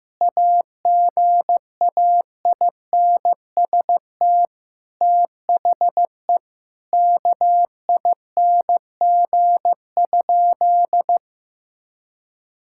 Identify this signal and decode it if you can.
Morse code, heard as a keyed tone, about 15 wpm, 700 Hz. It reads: AGAINST THE KING?
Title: Sentences from Top 1000 Words - Review Part 6 - 15wpm